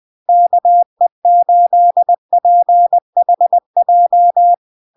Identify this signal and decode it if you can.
Morse code, heard as a keyed tone, about 20 wpm, 700 Hz. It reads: KE8PHJ
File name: LARK-FD-2022-Log-in-CW-20wpm-700-Hz